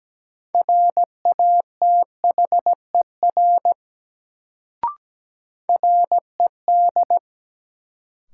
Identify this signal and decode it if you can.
Morse code, heard as a keyed tone, about 17 wpm, 700 Hz. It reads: RATHER RED
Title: Sentences from Top 700 Words - New Word Drill - 17wpm